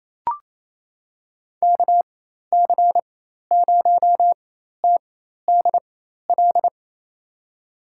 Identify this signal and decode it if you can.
Morse code, heard as a keyed tone, about 28 wpm, spaced as for 13 wpm, 700 Hz. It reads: KC0TDL
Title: USA Call Signs - 6 Characters - 4x Character Spacing - 28wpm